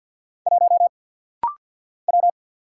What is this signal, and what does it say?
Morse code, heard as a keyed tone, about 50 wpm, 700 Hz. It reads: 1 W
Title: Single Letter-Number - Rapid-Fire 50wpm